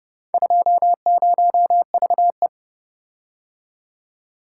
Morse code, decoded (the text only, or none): 20VE